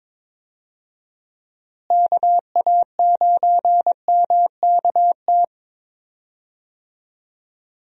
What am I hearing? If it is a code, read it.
Morse code, heard as a keyed tone, about 22 wpm, 700 Hz. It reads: KA9MKT
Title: USA Call Signs - 6 Characters - Repeated 2x - 22wpm